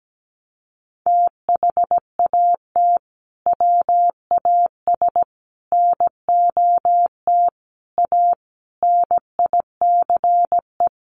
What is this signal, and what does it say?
Morse code, heard as a keyed tone, about 17 wpm, 700 Hz. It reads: THAT WAS NOT A NICE
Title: Sentences from Top 1200 Words 17wpm